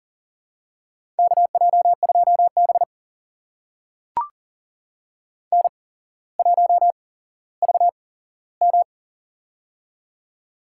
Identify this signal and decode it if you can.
Morse code, heard as a keyed tone, about 40 wpm, spaced as for 17 wpm, 700 Hz. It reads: KJ2B N1VM
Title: USA Call Signs - 4 Characters - 8x Character Spacing - 40wpm